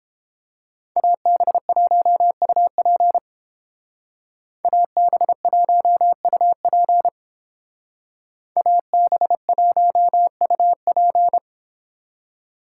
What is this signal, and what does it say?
Morse code, heard as a keyed tone, about 30 wpm, 700 Hz. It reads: AB1UP AB1UP AB1UP